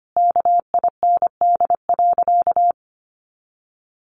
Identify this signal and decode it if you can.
Morse code, heard as a keyed tone, about 25 wpm, 700 Hz. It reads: KIND.
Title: Sentences from Top 500 Words 25wpm